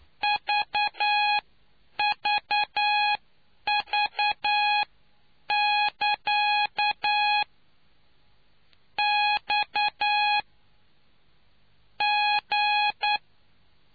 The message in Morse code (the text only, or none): VVV <KA> X G